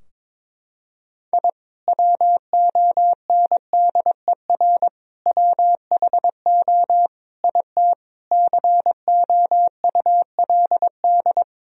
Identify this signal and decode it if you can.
Morse code, heard as a keyed tone, about 21 wpm, 700 Hz. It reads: I WONDER WHO IT COULD
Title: Sentences from Top 1000 Words - Review Part 3 - 22wpm